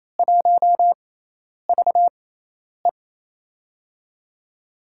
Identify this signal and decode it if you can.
Morse code, heard as a keyed tone, about 28 wpm, spaced as for 10 wpm, 700 Hz. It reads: 1VE
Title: USA Call Signs - 4 Characters - 6x Character Spacing - 28wpm